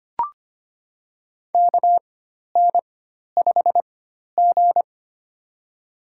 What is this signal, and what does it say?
Morse code, heard as a keyed tone, about 25 wpm, spaced as for 12 wpm, 700 Hz. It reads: KN5G
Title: USA Call Signs - 4 Characters - 4x Character Spacing - 25wpm